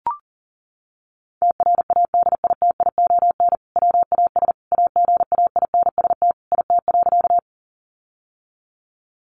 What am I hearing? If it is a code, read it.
Morse code, heard as a keyed tone, about 40 wpm, 700 Hz. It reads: TRADITION WAS AGAINST IT.
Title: Sentences from Top 1500 Words 40wpm